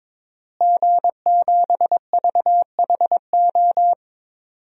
Morse code, heard as a keyed tone, about 22 wpm, 700 Hz. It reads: G7VHO
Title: DXCC-100 Call Signs - 5 Characters - Repeated 2x - 22wpm